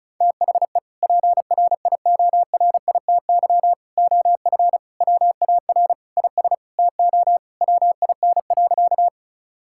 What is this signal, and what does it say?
Morse code, heard as a keyed tone, about 36 wpm, 700 Hz. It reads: THE PRIORITY OF WAR IS TO WIN.
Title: Sentences from Top 1600 Words 35wpm